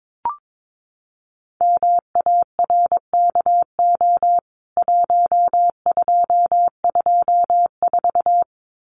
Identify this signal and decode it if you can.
Morse code, heard as a keyed tone, about 22 wpm, 700 Hz. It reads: MARKO 1224
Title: CWT Exchange 22wpm